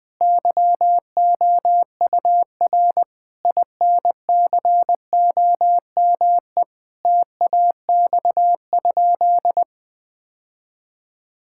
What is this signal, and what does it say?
Morse code, heard as a keyed tone, about 21 wpm, 700 Hz. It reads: YOUR INCOME TAX?
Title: Sentences from Top 1000 Words - Review Part 5 - 20wpm